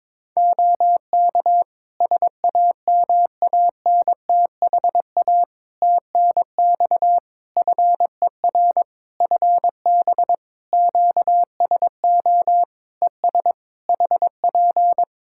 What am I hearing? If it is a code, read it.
Morse code, heard as a keyed tone, about 22 wpm, 700 Hz. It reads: OK SAMANTHA TNX FER FB QSO ES HP
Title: Practice QSO - Exchange 4 - Ending - 22wpm